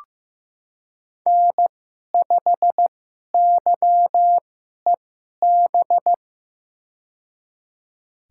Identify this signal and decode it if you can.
Morse code, heard as a keyed tone, about 15 wpm, spaced as for 11 wpm, 700 Hz. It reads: N5YEB
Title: USA Call Signs - 5 Characters - 2x Character Spacing - 15wpm